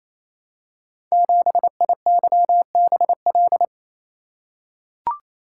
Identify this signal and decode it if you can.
Morse code, heard as a keyed tone, about 28 wpm, 700 Hz. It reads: 7IYBL
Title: Code Groups - 5 Letter:Numbers - 28wpm